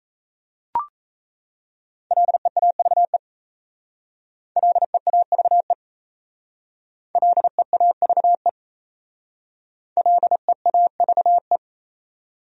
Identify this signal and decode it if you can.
Morse code, heard as a keyed tone, about 36 wpm, 700 Hz. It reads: LEAVE LEAVE LEAVE LEAVE